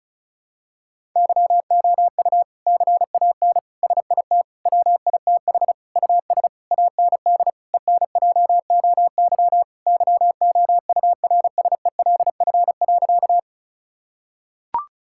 Morse code, heard as a keyed tone, about 34 wpm, 700 Hz. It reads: YOU CAN SIT WITH US AND ENJOY YOURSELF.
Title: Sentences from Top 1000 Words - Review Part 5 - 35wpm